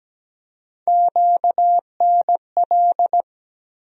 Morse code, heard as a keyed tone, about 17 wpm, 700 Hz. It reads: QNL
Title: USA Call Sign Postfixes - 3 Letters 17wpm